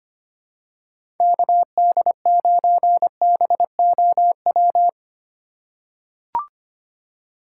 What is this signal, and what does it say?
Morse code, heard as a keyed tone, about 25 wpm, 700 Hz. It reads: KD9BOW
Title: USA Call Signs - 2x Character Spacing - 25wpm